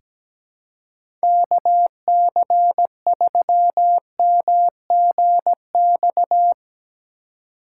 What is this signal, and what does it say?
Morse code, heard as a keyed tone, about 17 wpm, 700 Hz. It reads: KC3MGX
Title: USA Call Signs - 6 Characters - 3x Character Spacing - 17wpm